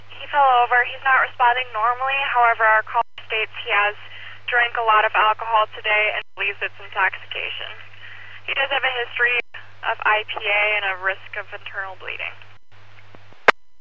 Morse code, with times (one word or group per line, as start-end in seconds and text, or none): none